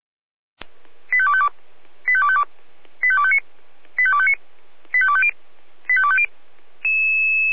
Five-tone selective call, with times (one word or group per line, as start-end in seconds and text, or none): 1.1-1.5 85252
2.1-2.4 85252
3.0-3.4 85259
4.0-4.3 85259
4.9-5.3 85250
5.9-6.3 85250